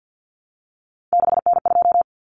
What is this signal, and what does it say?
Morse code, heard as a keyed tone, about 50 wpm, 700 Hz. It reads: 6N2
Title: Common Cut Numbers - 3-Digit 50wpm